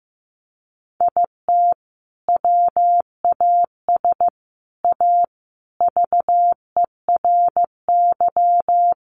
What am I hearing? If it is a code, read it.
Morse code, heard as a keyed tone, about 15 wpm, 700 Hz. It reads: IT WAS A VERY
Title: Sentences from Top 1400 Words 15wpm